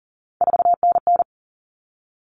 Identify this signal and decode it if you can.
Morse code, heard as a keyed tone, about 40 wpm, 700 Hz. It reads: 4NN
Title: Common Cut Numbers - 3-Digit 40wpm